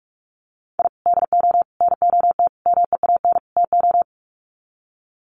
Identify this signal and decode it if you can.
Morse code, heard as a keyed tone, about 44 wpm, 700 Hz. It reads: I DO NOT MEAN TO
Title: Sets of 5 Words 45wpm